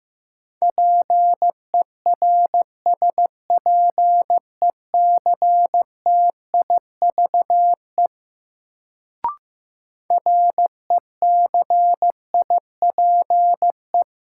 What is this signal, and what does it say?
Morse code, heard as a keyed tone, about 15 wpm, 700 Hz. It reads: PERSPECTIVE RECIPE
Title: Sentences from Top 1200 Words - New Word Drill Alt - 15wpm